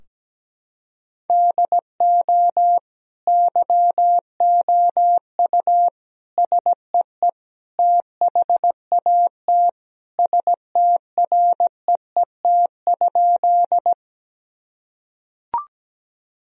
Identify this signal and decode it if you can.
Morse code, heard as a keyed tone, about 17 wpm, 700 Hz. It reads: DO YOU SEE THAT STREET?
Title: Sentences from Top 1000 Words - Review Part 3 - 17wpm